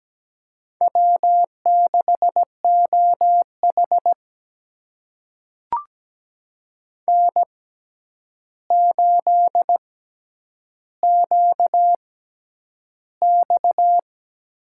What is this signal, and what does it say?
Morse code, heard as a keyed tone, about 17 wpm, spaced as for 9 wpm, 700 Hz. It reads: W6OH N8QX
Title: USA Call Signs - 4 Characters - 6x Character Spacing - 17wpm